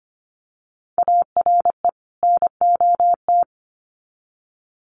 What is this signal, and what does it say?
Morse code, heard as a keyed tone, about 25 wpm, 700 Hz. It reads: ARE NOT
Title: Sets of 2 Words 25wpm